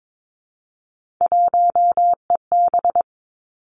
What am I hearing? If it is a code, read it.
Morse code, heard as a keyed tone, about 22 wpm, 700 Hz. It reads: 1EB
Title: Three Challenging Letter-Number Combinations 22wpm